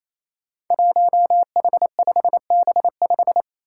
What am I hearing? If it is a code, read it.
Morse code, heard as a keyed tone, about 28 wpm, 700 Hz. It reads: 1H5B5
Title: Five Challenging Letter-Number Combinations 28wpm